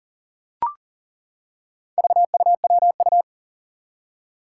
Morse code, heard as a keyed tone, about 40 wpm, 700 Hz. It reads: VUWU
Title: Four Challenging Letter-Number Combinations 40wpm